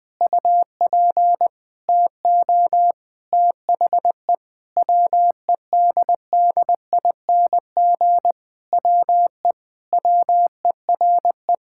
Morse code, heard as a keyed tone, about 20 wpm, 700 Hz. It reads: UP TO THE WEDDING WE WERE
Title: Sentences with 7-Word Phrases - Spoken Beginning - 20wpm